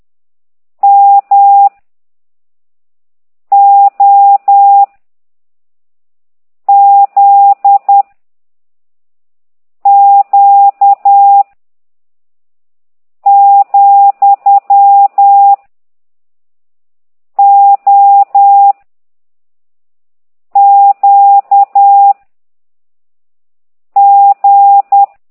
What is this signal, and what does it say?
Morse code, heard as a keyed tone, about 10 wpm, 800 Hz. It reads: M O Z Q , O Q G
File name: ct2idw / exerc13